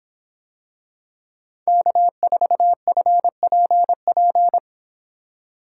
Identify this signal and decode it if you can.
Morse code, heard as a keyed tone, about 25 wpm, 700 Hz. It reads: K4FPP